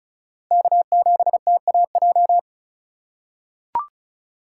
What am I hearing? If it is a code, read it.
Morse code, heard as a keyed tone, about 36 wpm, 700 Hz. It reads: K7TAJ
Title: USA Call Signs - 5 Characters - 3x Character Spacing - 35wpm